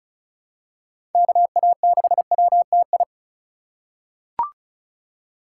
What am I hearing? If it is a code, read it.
Morse code, heard as a keyed tone, about 36 wpm, 700 Hz. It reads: KA6WTI
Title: USA Call Signs - 6 Characters - 4x Character Spacing - 35wpm